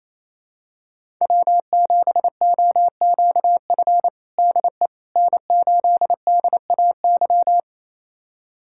Morse code, heard as a keyed tone, about 28 wpm, 700 Hz. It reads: W7OQF DE N8DAY
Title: USA Callsign DE Callsign - 5 Characters - 28wpm